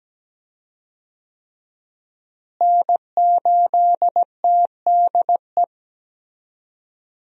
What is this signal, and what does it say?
Morse code, heard as a keyed tone, about 17 wpm, 700 Hz. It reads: N8TDE
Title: USA Call Signs - 5 Characters - Repeated 3x - 17wpm